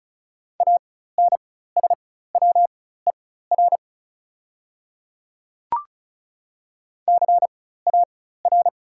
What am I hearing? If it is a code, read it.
Morse code, heard as a keyed tone, about 36 wpm, spaced as for 16 wpm, 700 Hz. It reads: ANSWER CAR
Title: 4x Character Spacing - Top 300 Words - 35wpm